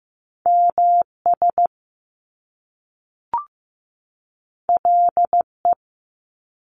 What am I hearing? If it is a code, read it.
Morse code, heard as a keyed tone, about 15 wpm, 700 Hz. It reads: MS LE
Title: Top Two Letter Combinations in Words 15wpm